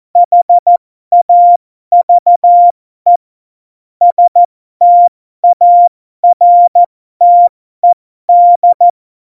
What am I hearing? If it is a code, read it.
Morse code, heard as a keyed tone, about 14 wpm, spaced as for 12 wpm, 700 Hz. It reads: HAVE STARTED
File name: Morsekurs / Witze